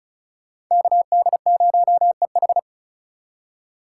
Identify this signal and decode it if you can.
Morse code, heard as a keyed tone, about 36 wpm, 700 Hz. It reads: KD0EH
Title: USA Call Signs - 5 Characters - 3x Character Spacing - 35wpm